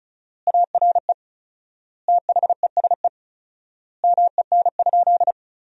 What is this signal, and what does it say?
Morse code, heard as a keyed tone, about 34 wpm, 700 Hz. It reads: ARE THESE MEN?
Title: 4x Word-Spacing - Sentences from Top 200 Words - 35wpm